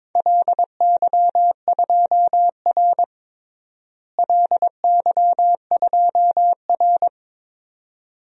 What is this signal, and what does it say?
Morse code, heard as a keyed tone, about 22 wpm, 700 Hz. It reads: LY2R LY2R
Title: DXCC-100 Call Signs - 4 Characters - Repeated 2x - 22wpm